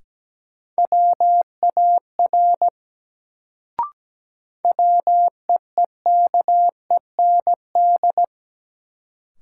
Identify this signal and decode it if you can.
Morse code, heard as a keyed tone, about 17 wpm, 700 Hz. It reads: WAR WEEKEND
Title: Sets of 5 Words - New Word Drill - 17wpm